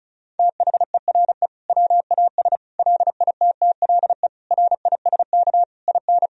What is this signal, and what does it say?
Morse code, heard as a keyed tone, about 36 wpm, 700 Hz. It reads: THERE WAS LITTLE RISK IN
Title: Sentences from Top 1000 Words - Review Part 4 - 35wpm